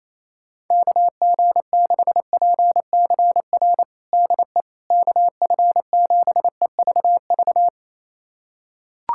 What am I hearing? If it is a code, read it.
Morse code, heard as a keyed tone, about 27 wpm, 700 Hz. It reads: KG6PCR DE KF7EVV
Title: USA Callsign DE Callsign - 6 Characters - 28wpm